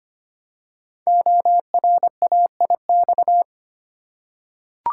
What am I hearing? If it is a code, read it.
Morse code, heard as a keyed tone, about 25 wpm, 700 Hz. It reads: ORAIX
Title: Code Groups - 5 Letters - 25wpm